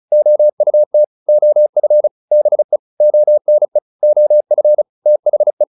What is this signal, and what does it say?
Morse code, heard as a keyed tone, about 34 wpm, 600 Hz. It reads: OUT OF BE ONE OF THE